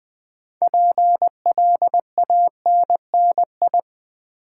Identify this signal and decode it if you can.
Morse code, heard as a keyed tone, about 20 wpm, 700 Hz. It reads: PLANNI
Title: Build A Word - 3 Letters At A Time - 8-Letter Words - 20wpm